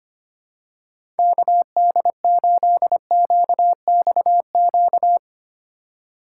Morse code, heard as a keyed tone, about 25 wpm, 700 Hz. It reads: KD8QXQ
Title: USA Call Signs - 6 Characters - 2x Character Spacing - 25wpm